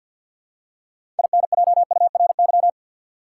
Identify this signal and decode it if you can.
Morse code, heard as a keyed tone, about 52 wpm, 700 Hz. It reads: INJURY
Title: Six Letter Words - 50wpm